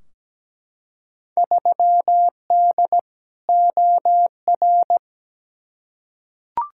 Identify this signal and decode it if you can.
Morse code, heard as a keyed tone, about 17 wpm, 700 Hz. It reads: 3D OR
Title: Field Day Exchanges - 17wpm